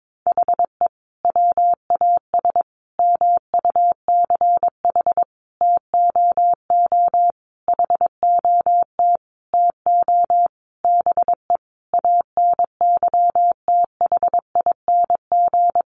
Morse code, heard as a keyed tone, about 22 wpm, 700 Hz. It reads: HE WAS MUCH TOO HOT TO BE ANYTHING
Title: Sentences from Top 1200 Words 22wpm